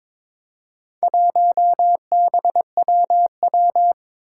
Morse code, heard as a keyed tone, about 23 wpm, 700 Hz. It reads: 1BWW
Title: Four Challenging Letter-Number Combinations 22wpm